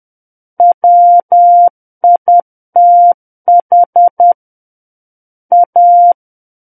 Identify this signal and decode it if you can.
Morse code, heard as a keyed tone, about 10 wpm, 700 Hz. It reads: WITH A
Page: War of the Worlds - 01-Chapter 1 - 10 WPM